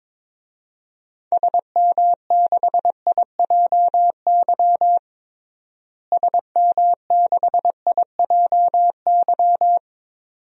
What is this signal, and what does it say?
Morse code, heard as a keyed tone, about 21 wpm, 700 Hz. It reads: SM6IJY SM6IJY
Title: DXCC-100 Call Signs - 6 Characters - Repeated 2x - 22wpm